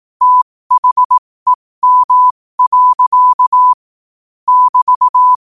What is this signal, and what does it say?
Morse code, heard as a keyed tone, about 18 wpm, spaced as for 15 wpm, 1000 Hz. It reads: THEM. <BT>
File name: qotdfiles / 15wpm